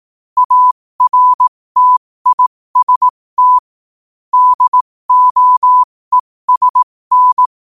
Morse code, heard as a keyed tone, about 18 wpm, spaced as for 15 wpm, 1000 Hz. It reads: ARTIST DOESN
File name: qotdfiles / 15wpm